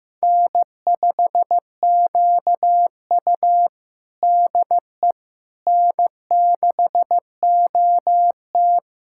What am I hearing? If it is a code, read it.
Morse code, heard as a keyed tone, about 16 wpm, 700 Hz. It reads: N5QU DE N6OT
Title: USA Callsign DE Callsign - 4 Characters - 15wpm